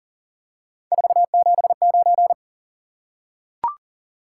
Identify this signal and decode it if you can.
Morse code, heard as a keyed tone, about 40 wpm, 700 Hz. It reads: 479
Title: Top 180 QSO Elements 40wpm